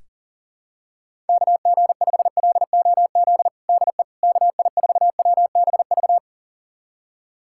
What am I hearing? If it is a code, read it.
Morse code, heard as a keyed tone, about 37 wpm, 700 Hz. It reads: KG5LOZ DE KI4WBV
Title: USA Callsign DE Callsign - 6 Characters - 40wpm